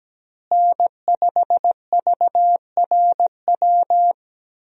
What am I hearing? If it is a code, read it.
Morse code, heard as a keyed tone, about 17 wpm, 700 Hz. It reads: N5VRW
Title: USA Call Signs - 5 Characters - Repeated 3x - 17wpm